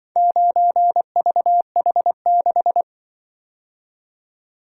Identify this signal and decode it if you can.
Morse code, heard as a keyed tone, about 24 wpm, 700 Hz. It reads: 9VH6
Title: Speed-Racing - Four Letter-Number Combinations - 20wpm